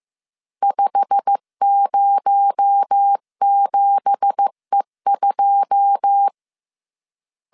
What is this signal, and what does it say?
Morse code, heard as a keyed tone, about 15 wpm, 800 Hz. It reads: 507E2
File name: CW / DET / 1 9 1500C S C 14wpm 3245